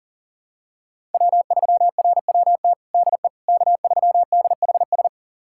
Morse code, heard as a keyed tone, about 40 wpm, 700 Hz. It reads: W3RWT DE K3DHS
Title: USA Callsign DE Callsign - 5 Characters - 40wpm